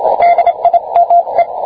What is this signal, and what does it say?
Morse code, heard as a keyed tone, about 26 wpm, 690 Hz, with radio noise. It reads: DIME